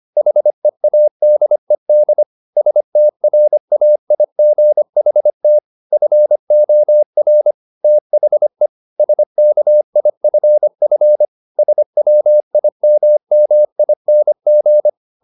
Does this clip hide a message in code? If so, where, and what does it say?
Yes: Morse code from 0.2 to 14.9 s, heard as a keyed tone, about 25 wpm, 600 Hz. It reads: HEADED STRAIGHT FOR THE SKIFF SWIMMING